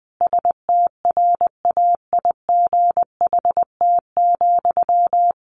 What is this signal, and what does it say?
Morse code, heard as a keyed tone, about 20 wpm, 700 Hz. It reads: STRAIGHT,
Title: Sentences from Top 1000 Words 20wpm